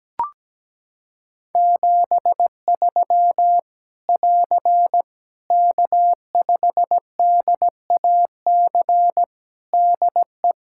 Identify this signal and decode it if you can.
Morse code, heard as a keyed tone, about 17 wpm, 700 Hz. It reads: 73 <AR> K5DAC DE
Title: Practice QSO - Exchange 4 - Ending - 17wpm